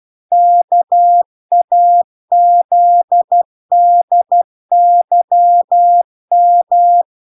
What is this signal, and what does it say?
Morse code, heard as a keyed tone, about 12 wpm, 700 Hz. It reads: KAZDYM